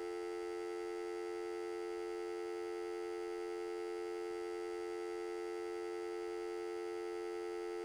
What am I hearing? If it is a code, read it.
Dial tone (North America), 350 + 440 Hz, with noise.
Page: Sony Pictures Sound Effects Series